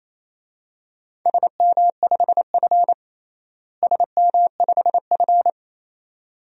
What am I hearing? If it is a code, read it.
Morse code, heard as a keyed tone, about 28 wpm, 700 Hz. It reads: SM5F SM5F
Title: DXCC-100 Call Signs - 4 Characters - Repeated 2x - 28wpm